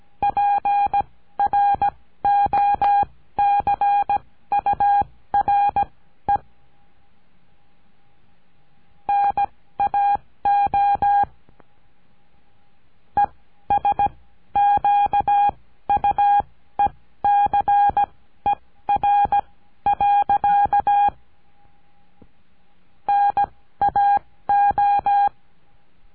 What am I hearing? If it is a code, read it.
Morse code, heard as a keyed tone, about 17 wpm, spaced as for 13 wpm, 810 Hz. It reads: PROCURE NAO ESQUECER. NAO
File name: CURSO DE TELEGRAFIA